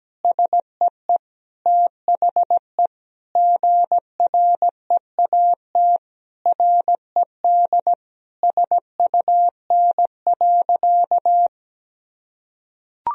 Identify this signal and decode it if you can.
Morse code, heard as a keyed tone, about 17 wpm, 700 Hz. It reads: SEE THE GREAT RED SUN.
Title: Sentences from Top 1000 Words - Review Part 2 - 17wpm